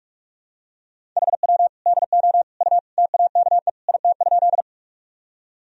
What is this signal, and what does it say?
Morse code, heard as a keyed tone, about 41 wpm, 700 Hz. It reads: HW DO U TAKE IT?
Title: Sentences from Top 200 Words and Abbreviations 45wpm